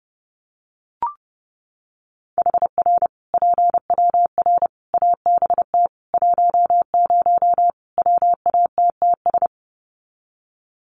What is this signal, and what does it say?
Morse code, heard as a keyed tone, about 30 wpm, 700 Hz. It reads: HR PWR ABT 10 WATTS
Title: Common QSO Phrases 30wpm